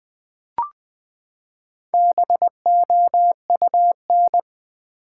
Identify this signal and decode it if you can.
Morse code, heard as a keyed tone, about 20 wpm, 700 Hz. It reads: BOUN
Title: Build A Word - 4 Letters At A Time - 8-Letter Words - 20wpm